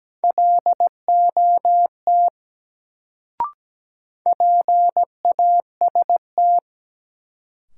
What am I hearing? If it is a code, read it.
Morse code, heard as a keyed tone, about 17 wpm, 700 Hz. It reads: LOT PAST
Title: Sets of 2 Words - New Word Drill - 17wpm